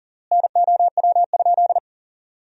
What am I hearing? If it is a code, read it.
Morse code, heard as a keyed tone, about 40 wpm, 700 Hz. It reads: NOW?
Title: 3x Word-Spacing - Sentences from Top 100 Words - 40wpm